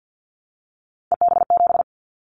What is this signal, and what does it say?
Morse code, heard as a keyed tone, about 50 wpm, 700 Hz. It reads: EB7
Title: Callsign Prefixes for Top 25 DXCC - 50wpm